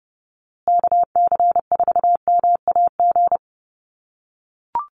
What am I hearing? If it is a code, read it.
Morse code, heard as a keyed tone, about 30 wpm, 700 Hz. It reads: KC4MAG
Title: USA Call Signs Repeated 3x 30wpm